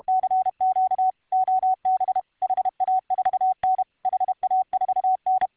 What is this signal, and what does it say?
Morse code, heard as a keyed tone, about 31 wpm, 750 Hz. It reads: CQ OB HA4N HA4N